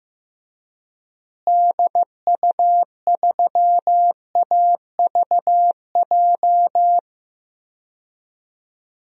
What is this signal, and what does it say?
Morse code, heard as a keyed tone, about 16 wpm, 700 Hz. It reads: DU3AVJ
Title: DXCC-100 Call Signs - 6 Characters - Repeated 2x - 15wpm